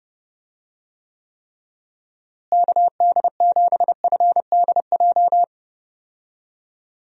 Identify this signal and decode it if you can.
Morse code, heard as a keyed tone, about 30 wpm, 700 Hz. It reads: KD7FDJ
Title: USA Call Signs - 6 Characters - Repeated 3x - 30wpm